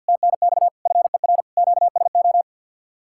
Morse code, heard as a keyed tone, about 60 wpm, 700 Hz. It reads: TNX FER QSO
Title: Speed-Racing Common QSO Phrases - 50wpm